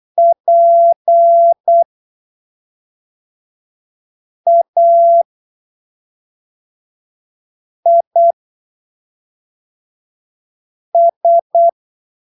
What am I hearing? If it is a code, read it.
Morse code, heard as a keyed tone, about 8 wpm, spaced as for 3 wpm, 680 Hz. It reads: PAIS
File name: arquivos / texto / texto 16 08ppm